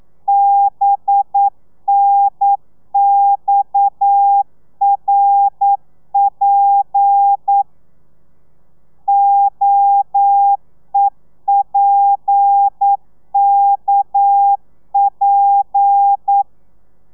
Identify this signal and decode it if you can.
Morse code, heard as a keyed tone, about 9 wpm, 800 Hz. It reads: BNXRP OEPKP
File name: telegrafia / i2tfj / Lez7 40